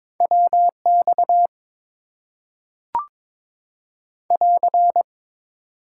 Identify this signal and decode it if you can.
Morse code, heard as a keyed tone, about 22 wpm, 700 Hz. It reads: WX <AR>
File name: Top 90 QSO Elements - 8x Character Spacing - 22wpm